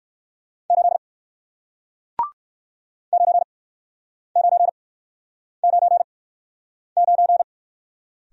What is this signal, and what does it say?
Morse code, heard as a keyed tone, about 60 wpm, spaced as for 9 wpm, 700 Hz. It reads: 8 9999